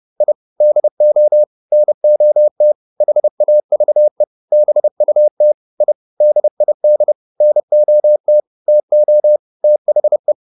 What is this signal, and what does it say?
Morse code, heard as a keyed tone, about 30 wpm, 600 Hz. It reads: I DO NOT HAVE BUT I DID NOT TO THE